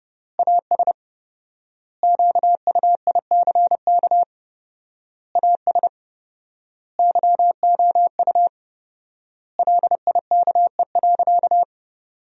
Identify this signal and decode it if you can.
Morse code, heard as a keyed tone, about 32 wpm, 700 Hz. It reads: AS QUICK AS YOU LIKE.
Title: 4x Word-Spacing - Sentences from Top 500 Words - 30wpm